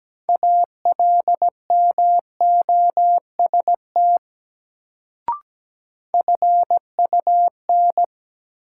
Morse code, heard as a keyed tone, about 17 wpm, 700 Hz. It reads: ALMOST FUN
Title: Sentences from Top 700 Words - New Word Drill - 17wpm